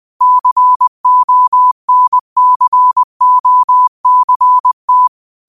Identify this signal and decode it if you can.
Morse code, heard as a keyed tone, about 20 wpm, 1000 Hz. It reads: CONCOCT
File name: 20wpm